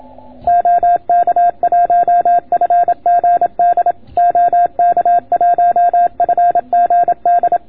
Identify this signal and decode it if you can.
Morse code, heard as a keyed tone, about 27 wpm, 690 Hz. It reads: OK1FGD OK1FGD